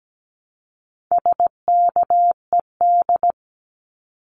Morse code, heard as a keyed tone, about 17 wpm, 700 Hz. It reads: SKED
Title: Abbreviations Repeated 3x - 17wpm